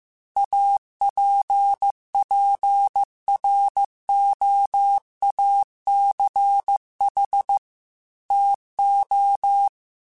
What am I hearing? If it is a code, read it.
Morse code, heard as a keyed tone, about 15 wpm, 800 Hz. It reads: APPROACH TO
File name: Technical Subjects / Technical Subjects-mp3s / Gates Intro @13